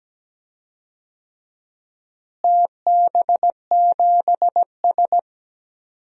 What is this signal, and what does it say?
Morse code, heard as a keyed tone, about 17 wpm, 700 Hz. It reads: TB7S
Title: DXCC-100 Call Signs - 4 Characters - Repeated 3x - 17wpm